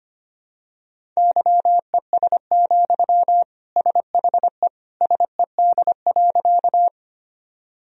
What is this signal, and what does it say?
Morse code, heard as a keyed tone, about 25 wpm, 700 Hz. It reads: YES, SHE SED.
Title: Sentences from Top 500 Words and Abbreviations 25wpm